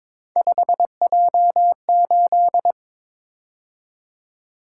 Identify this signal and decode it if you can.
Morse code, heard as a keyed tone, about 22 wpm, 700 Hz. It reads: 5J8
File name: practice / 22wpm / Three Letter-Number Combinations - 22wpm